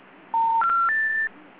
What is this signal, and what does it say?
Special information tone.